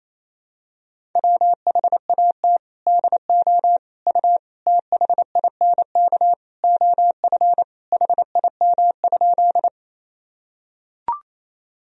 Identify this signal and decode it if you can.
Morse code, heard as a keyed tone, about 28 wpm, 700 Hz. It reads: WHAT DO U THINK OF HIM?
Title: Sentences from Top 200 Words and Abbreviations 28wpm